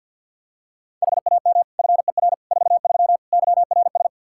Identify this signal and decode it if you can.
Morse code, heard as a keyed tone, about 52 wpm, 700 Hz. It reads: HAM FER 43 YRS
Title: Practice QSO - Exchange 3 - More Info - 50wpm